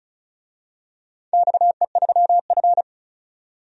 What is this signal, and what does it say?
Morse code, heard as a keyed tone, about 34 wpm, 700 Hz. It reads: XE3F
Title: DXCC-100 Call Signs - 4 Characters - 35wpm